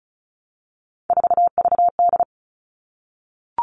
Morse code, heard as a keyed tone, about 34 wpm, 700 Hz. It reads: 4VD
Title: All Cut Numbers - 3-Digit 35wpm